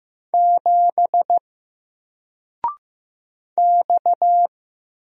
Morse code, heard as a keyed tone, about 15 wpm, 700 Hz. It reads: 7 X
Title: Single Letter-Number Alt 15wpm